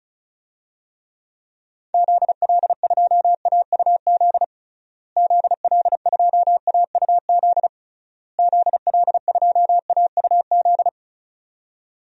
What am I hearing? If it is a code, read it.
Morse code, heard as a keyed tone, about 36 wpm, 700 Hz. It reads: ZL2AUZ ZL2AUZ ZL2AUZ
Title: DXCC-100 Call Signs - 6 Characters - Repeated 3x - 35wpm